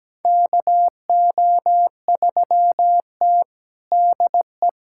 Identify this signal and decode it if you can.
Morse code, heard as a keyed tone, about 17 wpm, 700 Hz. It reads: KO3T DE
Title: USA Callsign DE Callsign - 4 Characters - 17wpm